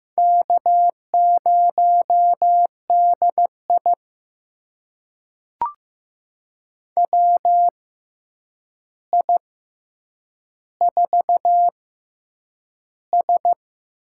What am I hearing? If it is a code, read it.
Morse code, heard as a keyed tone, about 15 wpm, spaced as for 8 wpm, 700 Hz. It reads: K0DI WI4S
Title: USA Call Signs - 4 Characters - 6x Character Spacing - 15wpm